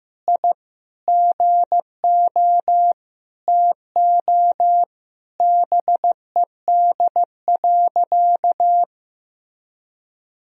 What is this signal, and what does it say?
Morse code, heard as a keyed tone, about 14 wpm, 700 Hz. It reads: I GO TO BED.
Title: Sentences from Top 1000 Words - Review Part 2 - 15wpm